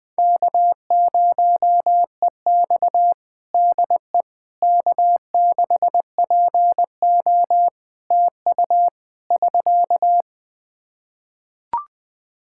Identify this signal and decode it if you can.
Morse code, heard as a keyed tone, about 20 wpm, 700 Hz. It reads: K0EX DE K6PO TU <SK>
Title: Practice QSO - Exchange 4 - Ending - 20wpm